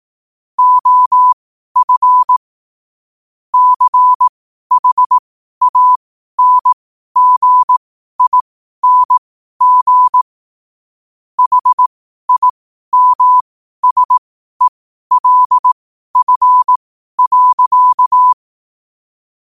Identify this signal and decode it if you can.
Morse code, heard as a keyed tone, about 18 wpm, spaced as for 12 wpm, 1000 Hz. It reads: OF CHANGING HIMSELF.
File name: qotdfiles / 12wpm